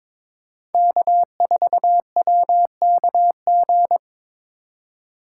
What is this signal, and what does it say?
Morse code, heard as a keyed tone, about 22 wpm, 700 Hz. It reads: K4WKG
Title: USA Call Signs - 5 Characters - 4x Character Spacing - 22wpm